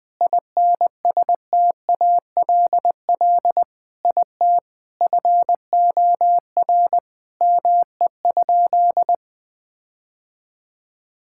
Morse code, heard as a keyed tone, about 20 wpm, 700 Hz. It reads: INSTALL IT FOR ME?
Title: Sentences from Top 1600 Words 20wpm